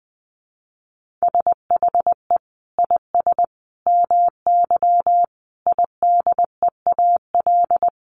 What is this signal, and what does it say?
Morse code, heard as a keyed tone, about 20 wpm, 700 Hz. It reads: SHE IS MY IDEAL
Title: Sentences from Top 1400 Words 20wpm